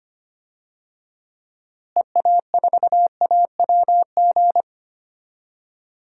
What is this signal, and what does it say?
Morse code, heard as a keyed tone, about 25 wpm, 700 Hz. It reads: EA4AWG
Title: DXCC-100 Call Signs - 6 Characters - Repeated 3x - 25wpm